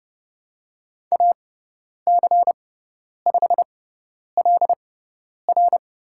Morse code, heard as a keyed tone, about 31 wpm, spaced as for 10 wpm, 700 Hz. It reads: AC5LR
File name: Lesson 17g - Letter u - Call Signs 30wpm